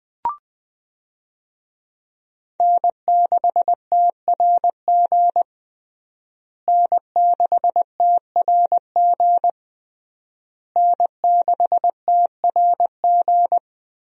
Morse code, heard as a keyed tone, about 21 wpm, 700 Hz. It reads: N6TRG N6TRG N6TRG
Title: USA Call Signs - 5 Characters - Repeated 3x - 20wpm